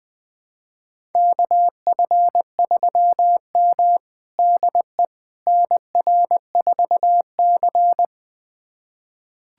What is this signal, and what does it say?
Morse code, heard as a keyed tone, about 20 wpm, 700 Hz. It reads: KF3M DE NR4C
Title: USA Callsign DE Callsign - 4 Characters - 20wpm